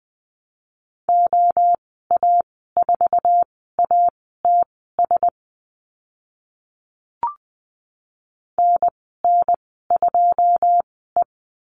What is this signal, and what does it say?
Morse code, heard as a keyed tone, about 20 wpm, spaced as for 15 wpm, 700 Hz. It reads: OA4ATS NN2E
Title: 082 Lesson 14g - Number 2 - Call Signs 20wpm